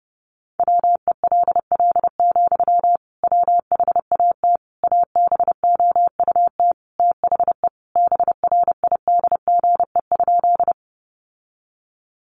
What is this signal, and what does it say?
Morse code, heard as a keyed tone, about 30 wpm, 700 Hz. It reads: WELL, WHAT ABOUT THE BRIDGE?
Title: Sentences from Top 1200 Words 30wpm